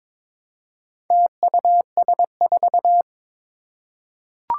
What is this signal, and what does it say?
Morse code, heard as a keyed tone, about 22 wpm, 700 Hz. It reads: TUS4
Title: Four Challenging Letter-Number Combinations 22wpm